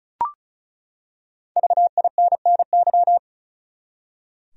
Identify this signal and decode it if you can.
Morse code, heard as a keyed tone, about 34 wpm, 700 Hz. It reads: VINNY
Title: USA Common Names - Max 5 Letters - 35wpm